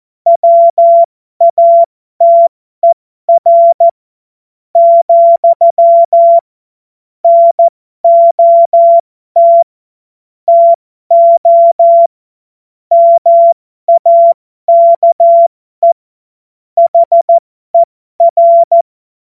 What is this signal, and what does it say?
Morse code, heard as a keyed tone, about 14 wpm, spaced as for 11 wpm, 680 Hz. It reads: WATER, NOT TO MAKE HER
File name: Morsekurs / Witze